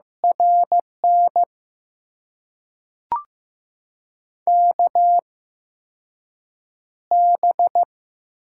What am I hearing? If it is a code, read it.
Morse code, heard as a keyed tone, about 15 wpm, spaced as for 6 wpm, 700 Hz. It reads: RN KB
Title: USA Call Signs - 6 Characters - 8x Character Spacing - 15wpm